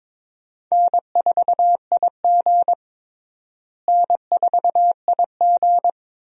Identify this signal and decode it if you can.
Morse code, heard as a keyed tone, about 22 wpm, 700 Hz. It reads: N4IG N4IG
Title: USA Call Signs - 4 Characters - Repeated 2x - 22wpm